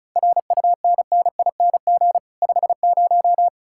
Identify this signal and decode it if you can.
Morse code, heard as a keyed tone, about 48 wpm, spaced as for 42 wpm, 700 Hz. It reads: RUNNING 50
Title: Speed-Racing Common QSO Phrases - 35wpm